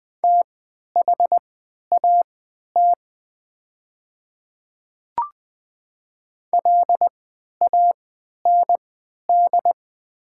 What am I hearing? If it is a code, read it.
Morse code, heard as a keyed tone, about 20 wpm, spaced as for 11 wpm, 690 Hz. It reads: THAT LAND
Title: 3x Character Spacing - Top 200 Words - 20wpm